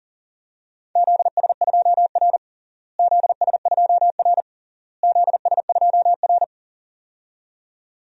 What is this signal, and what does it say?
Morse code, heard as a keyed tone, about 40 wpm, 700 Hz. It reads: ZS2R ZS2R ZS2R
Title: DXCC-100 Call Signs - 4 Characters - Repeated 3x - 40wpm